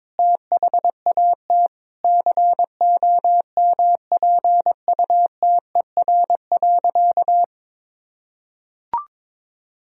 Morse code, heard as a keyed tone, about 22 wpm, 700 Hz. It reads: THAT COMPUTER.
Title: Sentences from Top 1000 Words - Review Part 3 - 22wpm